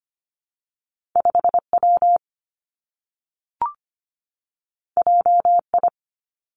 Morse code, heard as a keyed tone, about 25 wpm, 700 Hz. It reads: 5W JI
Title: Two Challenging Letter-Number Combinations 25wpm